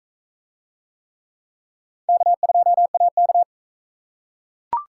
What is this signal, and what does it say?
Morse code, heard as a keyed tone, about 42 wpm, 700 Hz. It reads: K2AK